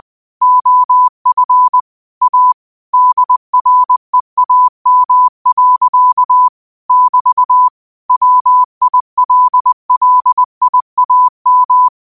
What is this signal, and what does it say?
Morse code, heard as a keyed tone, about 20 wpm, 1000 Hz. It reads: OF A DREAM. <BT> WILLIAM
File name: qotdfiles / 20wpm